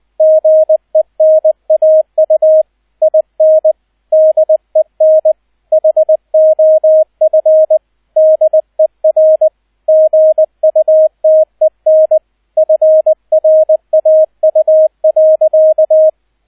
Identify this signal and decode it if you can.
Morse code, heard as a keyed tone, about 20 wpm, 630 Hz, with radio noise. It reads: GENAU IN DEN HOF DER GUTEN FRAU.
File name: Erster Streich 20wpm